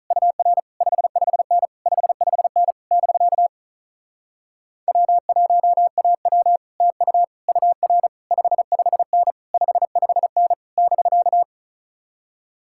Morse code, heard as a keyed tone, about 45 wpm, 700 Hz. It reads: UR 55N 55N <BK> W1AW TU UR 55N 55N <BK>